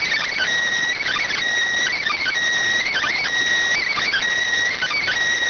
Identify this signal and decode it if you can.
Emergency Alert System (SAME) tones, heard as harsh buzzing bursts. It is heard from 1.0 to 5.0 s.